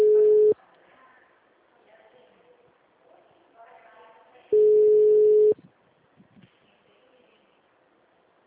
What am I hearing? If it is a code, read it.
Ringback tone (Europe, 425 Hz), with speech.